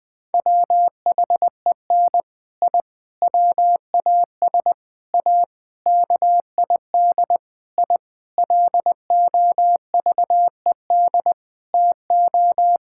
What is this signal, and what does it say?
Morse code, heard as a keyed tone, about 20 wpm, 700 Hz. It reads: WHEN I WAS A KID I LOVED TO
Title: Sentences with 5-Word Phrases - 20wpm